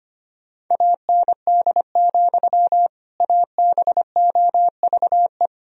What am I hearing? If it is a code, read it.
Morse code, heard as a keyed tone, about 25 wpm, 700 Hz. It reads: AND, ABOVE
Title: Sentences from Top 1000 Words - Review Part 2 - 25wpm